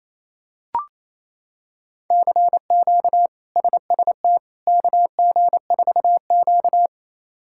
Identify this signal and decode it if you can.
Morse code, heard as a keyed tone, about 27 wpm, 700 Hz. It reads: CQ SST KG4Q
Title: K1USN SST QSO Practice - 28wpm